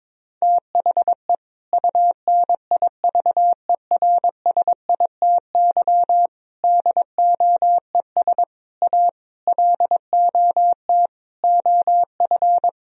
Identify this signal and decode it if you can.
Morse code, heard as a keyed tone, about 22 wpm, 700 Hz. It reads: THE UNIVERSITY DOES A LOT OF
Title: Sentences from Top 1000 Words - Review Part 4 - 22wpm